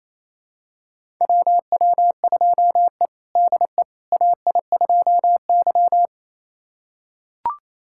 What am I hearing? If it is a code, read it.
Morse code, heard as a keyed tone, about 28 wpm, 700 Hz. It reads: WW2E DE AI2Y
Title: USA Callsign DE Callsign - 4 Characters - 28wpm